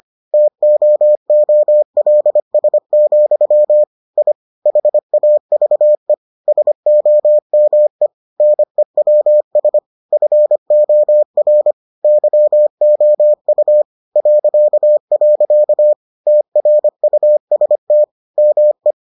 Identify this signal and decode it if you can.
Morse code, heard as a keyed tone, about 25 wpm, 600 Hz. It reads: TOOLS, I HAVE SOME NEWS FOR YOU .. TRUST ME